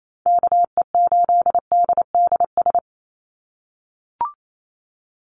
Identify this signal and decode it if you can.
Morse code, heard as a keyed tone, about 28 wpm, 700 Hz. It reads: KE8DDS
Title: USA Call Signs 28wpm